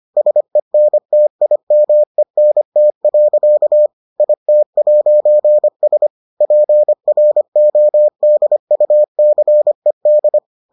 Morse code, heard as a keyed tone, about 25 wpm, 600 Hz. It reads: SENTIMENT. IT'S PRODUCED